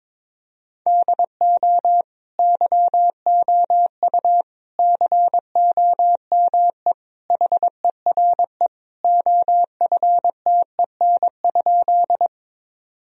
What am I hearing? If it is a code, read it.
Morse code, heard as a keyed tone, about 22 wpm, 700 Hz. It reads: DO YOU COME HERE OFTEN?
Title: Sentences from Top 1000 Words - Review Part 1 - 22wpm